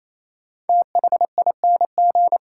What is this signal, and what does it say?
Morse code, heard as a keyed tone, about 28 wpm, 700 Hz. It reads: THING
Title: Sets of 3 Words - New Word Drill - 28wpm